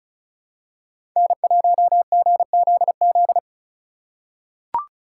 Morse code, heard as a keyed tone, about 34 wpm, 700 Hz. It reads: N1GZZ
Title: USA Call Signs - 5 Characters - 3x Character Spacing - 35wpm